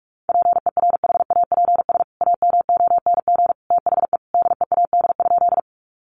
Morse code, heard as a keyed tone, about 44 wpm, 700 Hz. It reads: PERHAPS AMONG THE DEAD?
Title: Sentences from Top 1500 Words 45wpm